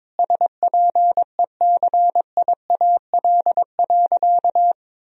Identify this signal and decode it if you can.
Morse code, heard as a keyed tone, about 22 wpm, 700 Hz. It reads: SPECIAL.
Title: Sentences from Top 1000 Words - Review Part 3 - 22wpm